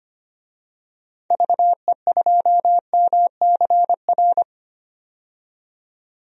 Morse code, heard as a keyed tone, about 25 wpm, 700 Hz. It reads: VE2MCR
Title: DXCC-100 Call Signs - 6 Characters - 25wpm